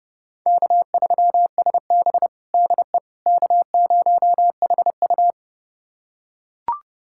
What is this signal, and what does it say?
Morse code, heard as a keyed tone, about 30 wpm, 700 Hz. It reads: K3SB DE K0HU
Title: USA Callsign DE Callsign - 4 Characters - 30wpm